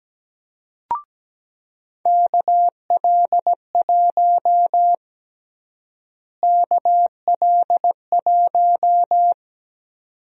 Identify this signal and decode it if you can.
Morse code, heard as a keyed tone, about 17 wpm, 700 Hz. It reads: KL1 KL1
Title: USA Call Sign Prefixes - Repeated 2x - 17wpm